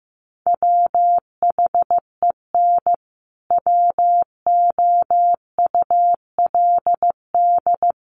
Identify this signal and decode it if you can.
Morse code, heard as a keyed tone, about 15 wpm, 700 Hz. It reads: WHEN WOULD
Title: Sentences from Top 1200 Words 15wpm